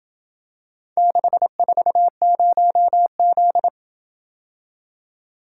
Morse code, heard as a keyed tone, about 26 wpm, 700 Hz. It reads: 640Z